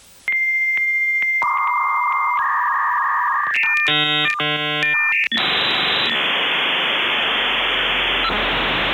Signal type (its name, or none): dial-up modem handshake